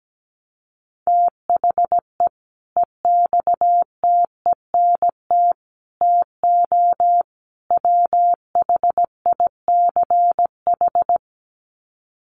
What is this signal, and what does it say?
Morse code, heard as a keyed tone, about 17 wpm, 700 Hz. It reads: THE EXTENT TO WHICH
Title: Sets of 4 Words 17wpm